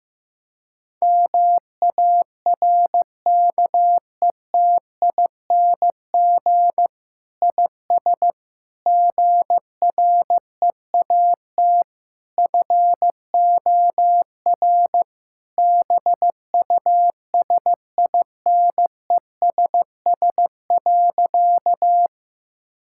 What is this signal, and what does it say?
Morse code, heard as a keyed tone, about 15 wpm, 700 Hz. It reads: MARKETING IS GREAT FOR BUSINESS.
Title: Sentences from Top 1000 Words - Review Part 5 - 15wpm